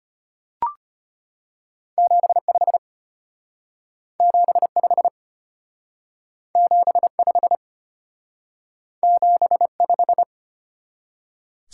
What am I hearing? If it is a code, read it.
Morse code, heard as a keyed tone, about 31 wpm, 700 Hz. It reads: 75 75 75 75